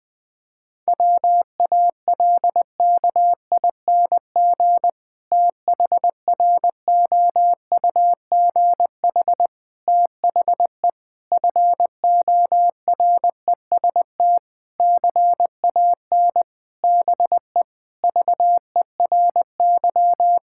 Morse code, heard as a keyed tone, about 20 wpm, 700 Hz. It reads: WALKING THROUGH THE FOREST CAN BE VERY
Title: Difficult Sentences with Hints - 20wpm